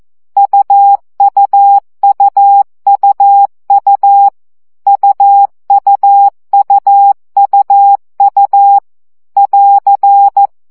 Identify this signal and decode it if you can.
Morse code, heard as a keyed tone, about 14 wpm, 800 Hz. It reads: UUUUU UUUUU <AR>
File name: CW gyakorlas u 12 baud